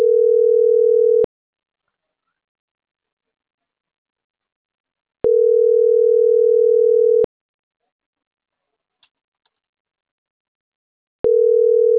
Ringback tone (North America), 440 + 480 Hz.